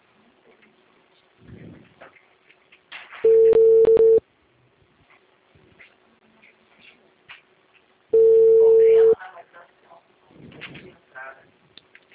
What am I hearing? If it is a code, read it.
Ringback tone (China, 450 Hz), with speech.
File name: out- -2002- - - .40438